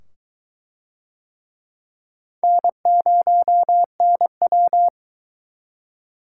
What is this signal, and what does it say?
Morse code, heard as a keyed tone, about 20 wpm, 700 Hz. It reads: N0NW